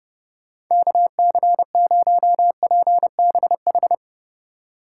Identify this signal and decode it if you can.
Morse code, heard as a keyed tone, about 30 wpm, 700 Hz. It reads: KC0PBH
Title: USA Call Signs - 6 Characters - 3x Character Spacing - 30wpm